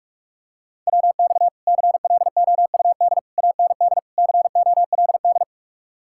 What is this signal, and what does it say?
Morse code, heard as a keyed tone, about 49 wpm, 700 Hz. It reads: WX CLOUD AND COLD